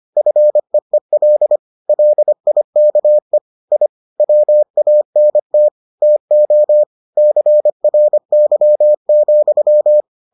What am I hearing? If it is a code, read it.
Morse code, heard as a keyed tone, about 25 wpm, 600 Hz. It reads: FEEL LIKE I WANT TO CRY,